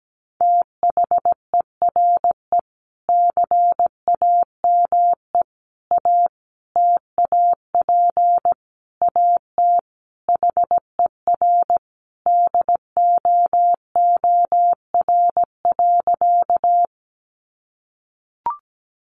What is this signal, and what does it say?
Morse code, heard as a keyed tone, about 17 wpm, 700 Hz. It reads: THERE CAME A TAP AT HER DOOR.
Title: Sentences from Top 1400 Words 17wpm